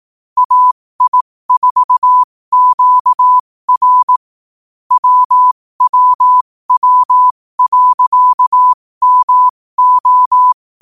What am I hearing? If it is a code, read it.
Morse code, heard as a keyed tone, about 18 wpm, spaced as for 15 wpm, 1000 Hz. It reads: AI4QR WWW.MO
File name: qotdfiles / 15wpm